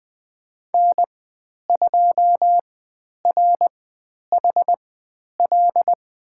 Morse code, heard as a keyed tone, about 20 wpm, spaced as for 10 wpm, 700 Hz. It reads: N2RHL
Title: 095 Lesson 16g - Letter c - Call Signs 20wpm